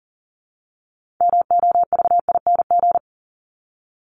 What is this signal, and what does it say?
Morse code, heard as a keyed tone, about 40 wpm, 700 Hz. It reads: MOVING
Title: ING Suffix - Top 100 Words with Base Words 40wpm